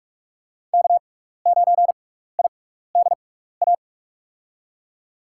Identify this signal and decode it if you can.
Morse code, heard as a keyed tone, about 46 wpm, spaced as for 16 wpm, 700 Hz. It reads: K9IDA
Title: USA Call Signs - 6x Character Spacing - 45wpm